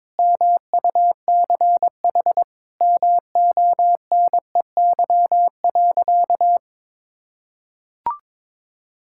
Morse code, heard as a keyed tone, about 22 wpm, 700 Hz. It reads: MUCH MONEY.
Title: Sentences from Top 1600 Words 22wpm